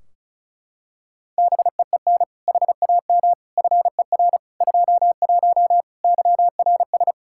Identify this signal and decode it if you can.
Morse code, heard as a keyed tone, about 36 wpm, 700 Hz. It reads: BEEN HAM FER 21 YRS
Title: Practice QSO - Exchange 3 - More Info - 35wpm